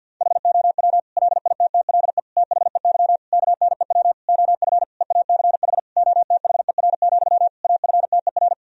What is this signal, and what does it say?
Morse code, heard as a keyed tone, about 51 wpm, 700 Hz. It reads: HOW LITTLE THEY KNEW OF EACH OTHER, AFTER
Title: Sentences from Top 1000 Words - Review Part 1 - 50wpm